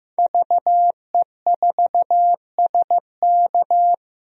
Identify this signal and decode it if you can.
Morse code, heard as a keyed tone, about 15 wpm, 700 Hz. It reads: VE4SK
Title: DXCC-100 Call Signs - 5 Characters - Repeated 2x - 15wpm